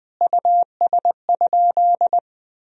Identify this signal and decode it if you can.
Morse code, heard as a keyed tone, about 20 wpm, 700 Hz. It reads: US?
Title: Sentences from Top 1000 Words - Review Part 3 - 20wpm